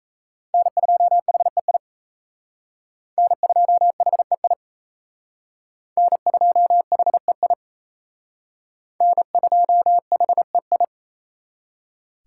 Morse code, heard as a keyed tone, about 34 wpm, 700 Hz. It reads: N2HEI N2HEI N2HEI N2HEI
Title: Speed Racing - USA 1x3 Call Signs 28wpm